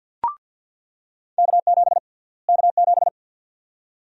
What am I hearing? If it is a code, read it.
Morse code, heard as a keyed tone, about 47 wpm, 700 Hz. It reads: K7 K7
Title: USA Call Sign Prefixes - Repeated 2x - 50wpm